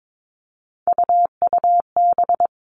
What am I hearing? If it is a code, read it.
Morse code, heard as a keyed tone, about 22 wpm, 700 Hz. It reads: UUB
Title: Three Challenging Letter-Number Combinations 22wpm